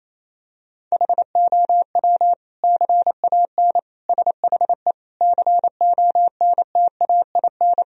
Morse code, heard as a keyed tone, about 27 wpm, 700 Hz. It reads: HOW CAN SHE CONTAIN
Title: Sentences from Top 1000 Words - Review Part 2 - 28wpm